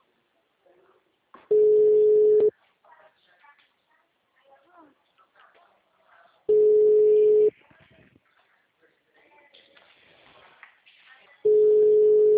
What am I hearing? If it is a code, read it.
Ringback tone (Europe, 425 Hz).
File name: out- -2007- - - .32522